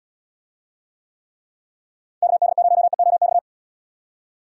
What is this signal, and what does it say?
Morse code, heard as a keyed tone, about 78 wpm, 700 Hz. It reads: KD0EOX